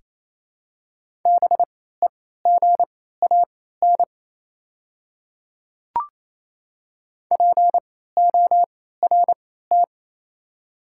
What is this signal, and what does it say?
Morse code, heard as a keyed tone, about 28 wpm, spaced as for 16 wpm, 700 Hz. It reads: BEGAN PORT
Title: 3x Character Spacing - Top 300 Words - 28wpm